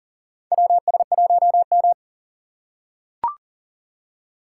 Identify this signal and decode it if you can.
Morse code, heard as a keyed tone, about 37 wpm, 700 Hz. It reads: WS1M
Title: USA Call Signs - 4 Characters - 3x Character Spacing - 40wpm